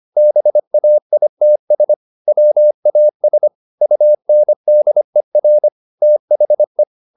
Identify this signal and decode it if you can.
Morse code, heard as a keyed tone, about 25 wpm, 600 Hz. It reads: BAITS WAS UNDER THE
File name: the old man and the sea part 1 25wpm